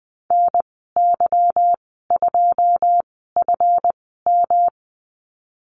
Morse code, heard as a keyed tone, about 20 wpm, spaced as for 15 wpm, 700 Hz. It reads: NY2FM
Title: 154 Lesson 25g - Letter p - Call Signs 20wpm